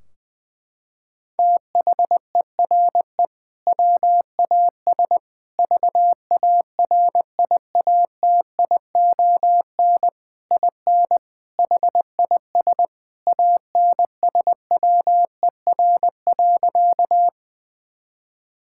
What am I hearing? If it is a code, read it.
Morse code, heard as a keyed tone, about 20 wpm, 700 Hz. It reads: THERE WAS VARIATION IN HIS ANSWER.
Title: Sentences from Top 1600 Words 20wpm